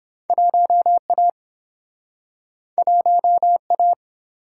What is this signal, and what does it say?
Morse code, heard as a keyed tone, about 28 wpm, 700 Hz. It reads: 1A 1A